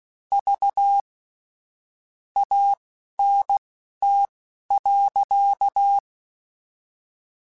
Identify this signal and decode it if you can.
Morse code, heard as a keyed tone, about 16 wpm, spaced as for 12 wpm, 800 Hz. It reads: V ANT.
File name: QSOs / QSOs-mp3s / WD0FBT 1@10